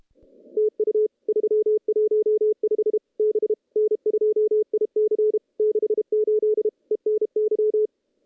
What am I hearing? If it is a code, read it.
Morse code, heard as a keyed tone, about 30 wpm, 410 Hz. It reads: TU 315 B N2IC 68 ENY
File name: contests / CW-SS-k5tr- -0000